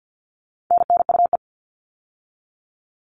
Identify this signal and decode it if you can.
Morse code, heard as a keyed tone, about 50 wpm, 700 Hz. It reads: NNUE
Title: All Cut Numbers - 4-Digit 50wpm